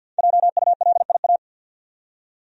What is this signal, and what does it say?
Morse code, heard as a keyed tone, about 53 wpm, 700 Hz. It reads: JULIA
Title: USA Common Names - 50wpm